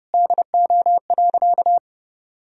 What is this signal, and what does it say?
Morse code, heard as a keyed tone, about 30 wpm, 700 Hz. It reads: DO.
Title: Sentences from Top 100 Words and Abbreviations 30wpm